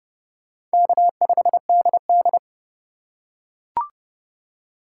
Morse code, heard as a keyed tone, about 29 wpm, 700 Hz. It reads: K5DD